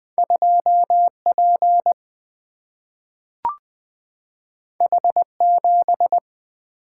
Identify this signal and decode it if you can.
Morse code, heard as a keyed tone, about 20 wpm, 700 Hz. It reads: 2 P H 7
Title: Single-Letter Single-Number 20wpm